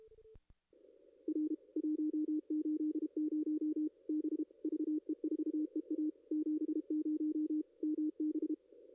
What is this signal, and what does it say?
Morse code, heard as a keyed tone, about 32 wpm, 310 Hz. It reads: R 180 B VE4EA 70 MB
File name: contests / ss-cw-k5tr-op-n5rz -2300